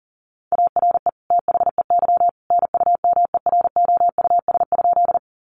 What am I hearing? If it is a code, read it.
Morse code, heard as a keyed tone, about 40 wpm, 700 Hz. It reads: ARE THEY NUMEROUS?
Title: Sentences from Top 1400 Words 40wpm